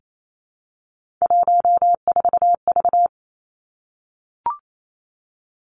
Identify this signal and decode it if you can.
Morse code, heard as a keyed tone, about 28 wpm, 700 Hz. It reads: 14V
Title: Three Challenging Letter-Number Combinations 28wpm